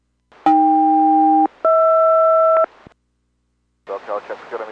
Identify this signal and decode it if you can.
Aircraft SELCAL, heard as two pairs of tones, with speech. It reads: AK-HR